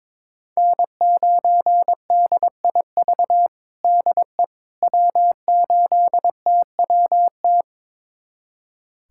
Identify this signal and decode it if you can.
Morse code, heard as a keyed tone, about 22 wpm, 700 Hz. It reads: N9DIV DE W8TWT
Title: USA Callsign DE Callsign - 5 Characters - 22wpm